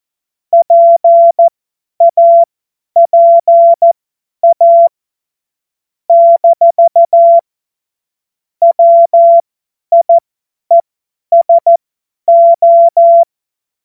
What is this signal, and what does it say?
Morse code, heard as a keyed tone, about 14 wpm, spaced as for 10 wpm, 680 Hz. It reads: PAPA - WIESO
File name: Morsekurs / Witze / Witz 02. Februar 2021, 50 F70